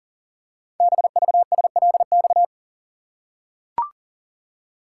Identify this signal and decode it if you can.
Morse code, heard as a keyed tone, about 39 wpm, 700 Hz. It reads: BVSLX
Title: Code Groups - 5 Letter:Numbers - 40wpm